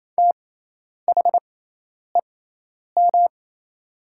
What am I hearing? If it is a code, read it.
Morse code, heard as a keyed tone, about 28 wpm, spaced as for 10 wpm, 700 Hz. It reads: THEM
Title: 6x Character Spacing - Top 200 Words - 28wpm